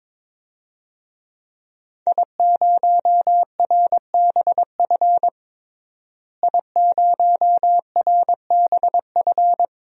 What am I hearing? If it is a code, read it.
Morse code, heard as a keyed tone, about 22 wpm, 700 Hz. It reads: I0RBF I0RBF
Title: DXCC-100 Call Signs - 5 Characters - Repeated 2x - 22wpm